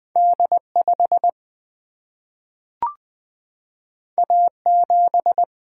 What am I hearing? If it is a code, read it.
Morse code, heard as a keyed tone, about 20 wpm, 700 Hz. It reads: D 5 A 7
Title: Single-Letter Single-Number 20wpm